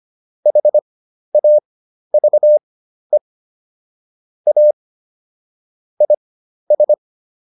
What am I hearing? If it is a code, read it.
Morse code, heard as a keyed tone, about 25 wpm, spaced as for 12 wpm, 600 Hz. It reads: HAVE A IS